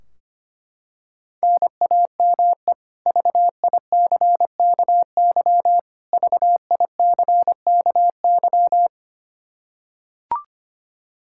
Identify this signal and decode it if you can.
Morse code, heard as a keyed tone, about 25 wpm, 700 Hz. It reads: NAME VICKY VICKY
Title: K1USN SST QSO Practice 25wpm